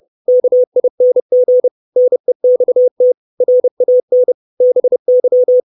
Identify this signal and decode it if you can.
Morse code, heard as a keyed tone, about 30 wpm, 500 Hz. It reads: KING NEXT RAN BY
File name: 632 Words 30WPM